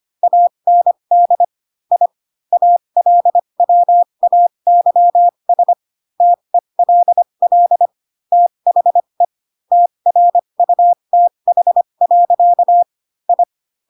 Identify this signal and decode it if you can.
Morse code, heard as a keyed tone, about 25 wpm, spaced as for 22 wpm, 700 Hz. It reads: AND I ALWAYS TELL THE TRUTH. I